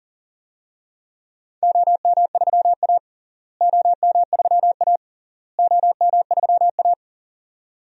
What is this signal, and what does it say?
Morse code, heard as a keyed tone, about 40 wpm, 700 Hz. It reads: OM3A OM3A OM3A
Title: DXCC-100 Call Signs - 4 Characters - Repeated 3x - 40wpm